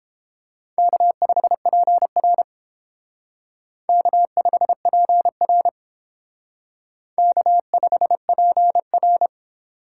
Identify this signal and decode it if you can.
Morse code, heard as a keyed tone, about 30 wpm, 700 Hz. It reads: K5PR K5PR K5PR